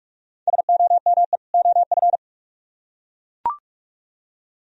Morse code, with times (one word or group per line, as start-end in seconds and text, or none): 0.5-1.4 SOME
1.5-2.2 OF